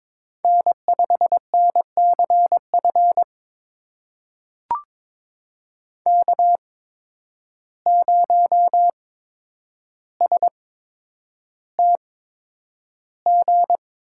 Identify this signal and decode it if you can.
Morse code, heard as a keyed tone, about 22 wpm, spaced as for 9 wpm, 700 Hz. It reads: N5NCF K0STG
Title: USA Call Signs - 8x Character Spacing - 22wpm